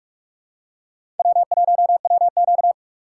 Morse code, heard as a keyed tone, about 44 wpm, 700 Hz. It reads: W1WQ